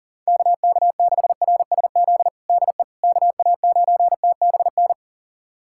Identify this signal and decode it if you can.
Morse code, heard as a keyed tone, about 40 wpm, 700 Hz. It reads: KK6RSZ DE KA9TBN
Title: USA Callsign DE Callsign 40wpm